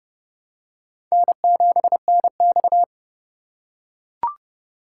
Morse code, heard as a keyed tone, about 30 wpm, 700 Hz. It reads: N7NX
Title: USA Call Signs - 4 Characters - 2x Character Spacing - 30wpm